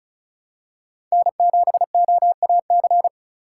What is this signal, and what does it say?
Morse code, heard as a keyed tone, about 34 wpm, 700 Hz. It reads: N7OAC
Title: USA Call Signs - 5 Characters - 3x Character Spacing - 35wpm